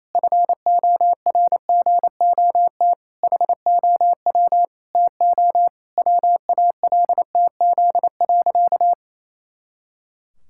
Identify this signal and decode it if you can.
Morse code, heard as a keyed tone, about 28 wpm, 700 Hz. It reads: FORGOT HOW TO WALTZ.
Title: Pangram Challenge v2 - 28wpm